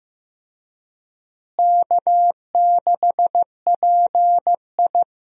Morse code, heard as a keyed tone, about 15 wpm, 700 Hz. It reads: K6PI
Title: USA Call Signs - 4 Characters - Repeated 2x - 15wpm